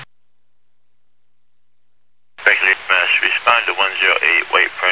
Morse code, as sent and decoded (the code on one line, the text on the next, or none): none